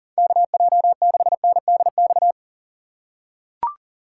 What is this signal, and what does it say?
Morse code, heard as a keyed tone, about 40 wpm, 700 Hz. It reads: KJ6NDX
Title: USA Call Signs - 6 Characters - 2x Character Spacing - 40wpm